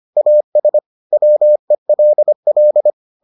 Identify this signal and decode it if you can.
Morse code, heard as a keyed tone, about 25 wpm, 600 Hz. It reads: AS WELL